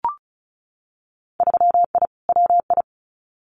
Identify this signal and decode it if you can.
Morse code, heard as a keyed tone, about 36 wpm, 700 Hz. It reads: 3I WI
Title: Winter Field Day Exchanges 35wpm